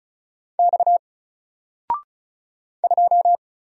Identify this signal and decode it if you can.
Morse code, heard as a keyed tone, about 35 wpm, 700 Hz. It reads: X 2
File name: Single-Letter-Number-35wpm